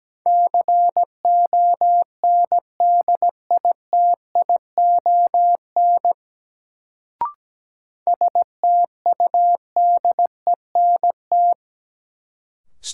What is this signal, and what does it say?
Morse code, heard as a keyed tone, about 17 wpm, 700 Hz. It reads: CONDITION STUDENT
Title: Sentences from Top 1000 Words - New Word Drill - 17wpm